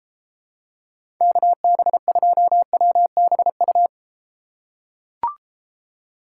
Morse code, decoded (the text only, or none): KB2WBU